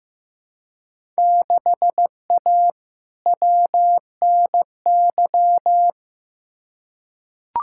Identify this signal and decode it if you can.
Morse code, heard as a keyed tone, about 15 wpm, 700 Hz. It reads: 6A WNY
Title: Field Day Exchanges Encore - 15wpm